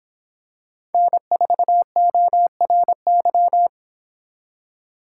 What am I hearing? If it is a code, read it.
Morse code, heard as a keyed tone, about 25 wpm, 700 Hz. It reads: N4ORY